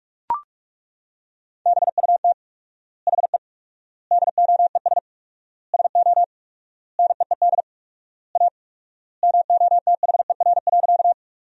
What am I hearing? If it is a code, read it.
Morse code, heard as a keyed tone, about 42 wpm, 700 Hz. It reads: BUT HE DOES SO NEED A MOTHER.
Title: 4x Word-Spacing - Sentences from Top 200 Words - 45wpm